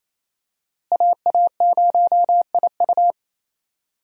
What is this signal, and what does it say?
Morse code, heard as a keyed tone, about 28 wpm, 700 Hz. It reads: AA0IU
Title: USA Call Signs - 5 Characters - Repeated 2x - 28wpm